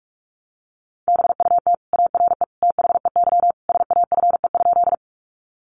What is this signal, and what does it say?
Morse code, heard as a keyed tone, about 44 wpm, 700 Hz. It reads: BUT ARE THEY SAFE?
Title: Sentences from Top 900 Words 45wpm